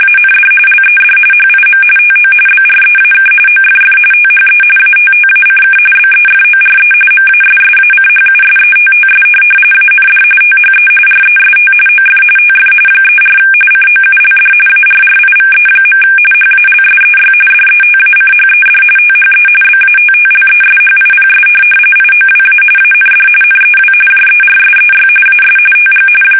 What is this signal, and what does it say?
RTTY, about 45.45 baud. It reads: AND LED HIM TO BREAK OFF THE ATTACK BEFORE MASSACHUSETTS AND THE REST OF TF 34 COULD ARRIVE. HALSEY DETACHED THE BATTLESHIPS IOWA AND NEW JERSEY AS TG
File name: fsk850 / USS-Massachusetts